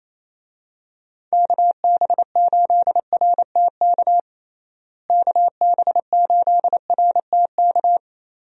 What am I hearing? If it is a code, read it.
Morse code, heard as a keyed tone, about 28 wpm, 700 Hz. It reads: KB8RTK KB8RTK
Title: USA Call Signs - 6 Characters - Repeated 2x - 28wpm